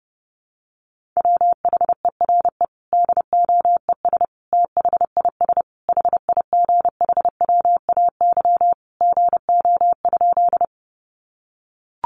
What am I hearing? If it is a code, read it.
Morse code, heard as a keyed tone, about 30 wpm, 700 Hz. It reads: WHERE DOES THIS HIGHWAY GO?
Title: Sentences from Top 1400 Words 30wpm